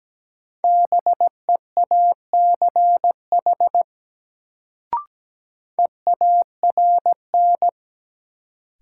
Morse code, heard as a keyed tone, about 17 wpm, 700 Hz. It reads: BEACH EARN
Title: Sentences from Top 1500 Words - New Word Drill Alt - 17wpm